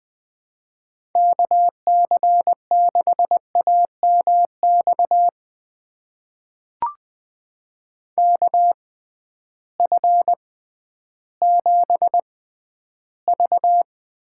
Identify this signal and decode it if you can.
Morse code, heard as a keyed tone, about 21 wpm, spaced as for 12 wpm, 700 Hz. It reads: KC6AMX KF7V
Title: USA Call Signs - 6x Character Spacing - 20wpm